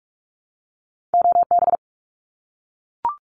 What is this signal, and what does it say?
Morse code, heard as a keyed tone, about 44 wpm, 700 Hz. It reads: OB
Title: Abbreviations 45wpm